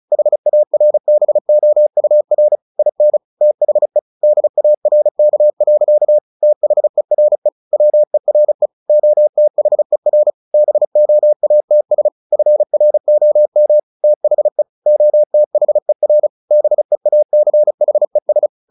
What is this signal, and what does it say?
Morse code, heard as a keyed tone, about 34 wpm, 600 Hz. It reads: HARBOUR IN THE DARK. THERE WERE OTHER BOATS FROM THE OTHER BEACHES
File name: the old man and the sea / the old man and the sea 1 35wpm